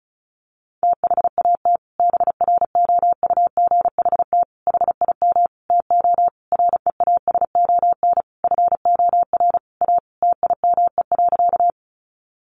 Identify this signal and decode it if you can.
Morse code, heard as a keyed tone, about 34 wpm, 700 Hz. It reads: THAT BROUGHT HIM TO REASON FOR A TIME.
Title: Sentences from Top 700 Words 35wpm